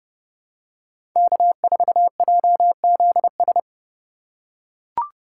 Morse code, heard as a keyed tone, about 32 wpm, 700 Hz. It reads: K4JZS
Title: USA Call Signs - 5 Characters - 3x Character Spacing - 30wpm